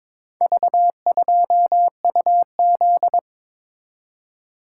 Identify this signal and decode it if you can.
Morse code, heard as a keyed tone, about 22 wpm, 700 Hz. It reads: V2UZ
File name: practice / 22wpm / Four Letter-Number Combinations - 22wpm